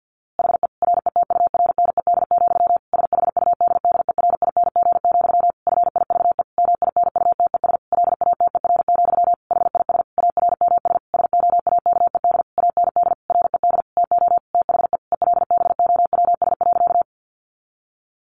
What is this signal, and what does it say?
Morse code, heard as a keyed tone, about 50 wpm, 700 Hz. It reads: HE RETURNED, SHUDDERING, FIVE MINUTES LATER, HIS ARMS SOAKED AND RED TO THE ELBOWS.
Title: A Sound Of Thunder 50wpm